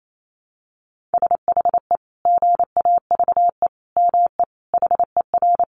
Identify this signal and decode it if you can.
Morse code, heard as a keyed tone, about 28 wpm, 700 Hz. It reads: SHE GAVE ME HER
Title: Sentences from Top 500 Words 28wpm